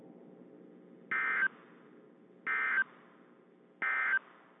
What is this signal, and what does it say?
Emergency Alert System (SAME) tones, heard as harsh buzzing bursts. It is heard from 1.1 to 4.2 s.
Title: WZ2523 - NOAA Weather Radio